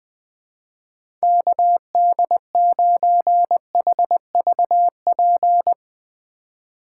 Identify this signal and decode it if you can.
Morse code, heard as a keyed tone, about 20 wpm, 700 Hz. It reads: KD9HVP
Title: Speed Racing - USA 2x3 Call Signs 20wpm